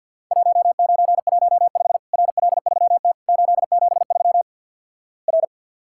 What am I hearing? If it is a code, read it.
Morse code, heard as a keyed tone, about 51 wpm, 700 Hz. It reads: 1915 RL3T 873 R
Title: Worked All Europe DX Contest - QTC Exchange - 50wpm